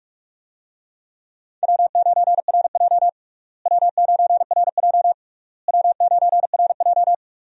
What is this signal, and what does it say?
Morse code, heard as a keyed tone, about 42 wpm, 700 Hz. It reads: W9RJ W9RJ W9RJ
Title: USA Call Signs - 4 Characters - Repeated 3x - 45wpm